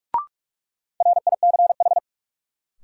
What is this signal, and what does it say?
Morse code, heard as a keyed tone, about 47 wpm, 700 Hz. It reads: RICH
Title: Sentences from Top 1100 Words - New Word Drill - 45wpm